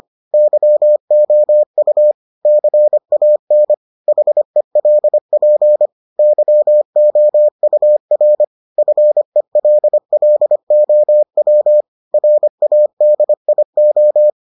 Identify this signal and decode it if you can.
Morse code, heard as a keyed tone, about 25 wpm, 600 Hz. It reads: YOU CAN HELP YOUR FELLOW RADIO